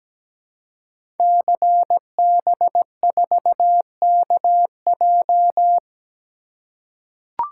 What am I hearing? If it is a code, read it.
Morse code, heard as a keyed tone, about 17 wpm, 700 Hz. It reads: CB4KJ
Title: Code Groups - 5 Letter:Numbers - 17wpm